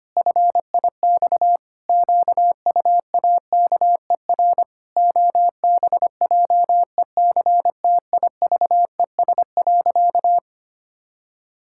Morse code, heard as a keyed tone, about 25 wpm, 700 Hz. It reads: FIX QUAKER OBJECTIVES.
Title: Pangram Challenge v2 - 25wpm